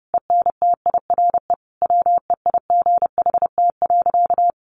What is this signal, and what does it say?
Morse code, heard as a keyed tone, about 30 wpm, 700 Hz. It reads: ENTIRE WEIGHT.
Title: Sentences from Top 800 Words 30wpm